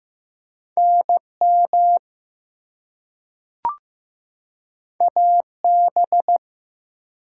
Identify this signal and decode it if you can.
Morse code, heard as a keyed tone, about 15 wpm, 700 Hz. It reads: NM AB
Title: US States and Canadian Provinces 15wpm